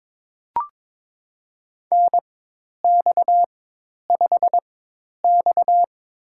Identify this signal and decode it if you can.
Morse code, heard as a keyed tone, about 22 wpm, spaced as for 10 wpm, 700 Hz. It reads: NX5X
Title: USA Call Signs - 4 Characters - 4x Character Spacing - 22wpm